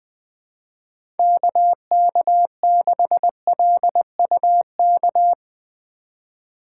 Morse code, heard as a keyed tone, about 20 wpm, 700 Hz. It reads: KK6LUK
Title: USA Call Signs - 8x Character Spacing - 20wpm